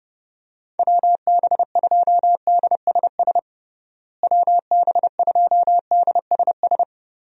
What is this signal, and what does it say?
Morse code, heard as a keyed tone, about 32 wpm, 700 Hz. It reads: WB2DSS WB2DSS
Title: USA Call Signs - 6 Characters - Repeated 2x - 30wpm